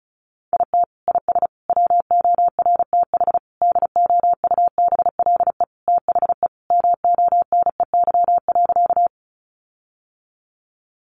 Morse code, heard as a keyed tone, about 34 wpm, 700 Hz. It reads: IT IS WORTH DOUBLE THE MONEY.
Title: Sentences from Top 1400 Words 35wpm